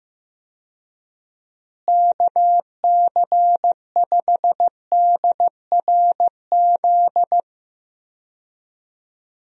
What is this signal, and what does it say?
Morse code, heard as a keyed tone, about 15 wpm, 700 Hz. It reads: KC5DRZ
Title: USA Call Signs - 6 Characters - Repeated 2x - 15wpm